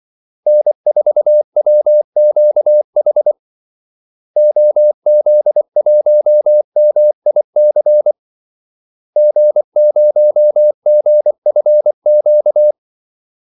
Morse code, heard as a keyed tone, about 24 wpm, 600 Hz. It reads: N4WQH OZ1MIC G0GFQ
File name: Callsigns-24wpm